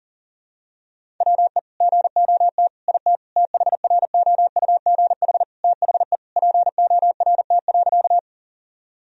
Morse code, heard as a keyed tone, about 40 wpm, 700 Hz. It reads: WE GOT IT THROUGH THE PORT.
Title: Sentences from Top 1000 Words - Review Part 1 - 40wpm